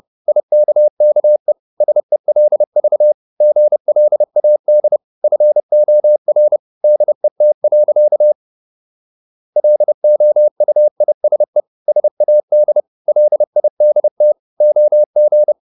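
Morse code, heard as a keyed tone, about 30 wpm, 600 Hz. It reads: IKKE SELV GLAD FOR DET. LOUISE SAD LIDT OG
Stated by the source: High-speed Morse (QRQ) at 30 wpm.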